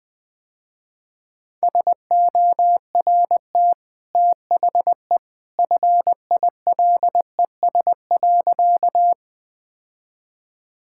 Morse code, heard as a keyed tone, about 20 wpm, 700 Hz. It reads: SORT THE FILES.
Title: Easy Sentences with Hints - 20wpm